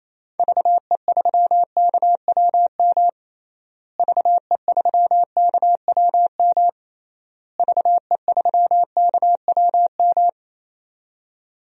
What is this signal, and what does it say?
Morse code, heard as a keyed tone, about 27 wpm, 700 Hz. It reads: VE3KWM VE3KWM VE3KWM
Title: DXCC-100 Call Signs - 6 Characters - Repeated 3x - 28wpm